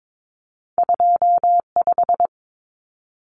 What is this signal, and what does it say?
Morse code, heard as a keyed tone, about 22 wpm, 700 Hz. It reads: 25
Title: Common Cut Numbers - 2-Digit 22wpm